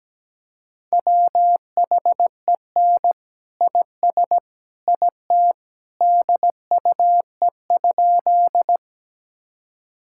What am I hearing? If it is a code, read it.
Morse code, heard as a keyed tone, about 17 wpm, 700 Hz. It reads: WHEN IS IT DUE?
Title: Sentences from Top 1000 Words - Review Part 6 - 17wpm